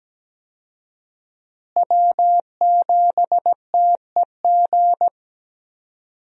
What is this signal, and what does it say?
Morse code, heard as a keyed tone, about 17 wpm, 700 Hz. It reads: W7TEG
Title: USA Call Signs - 5 Characters - Repeated 3x - 17wpm